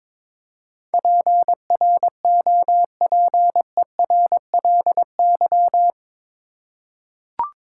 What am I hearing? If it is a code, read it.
Morse code, heard as a keyed tone, about 22 wpm, 700 Hz. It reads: PROPERLY
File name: practice / 22wpm / Eight Letter Words - 22wpm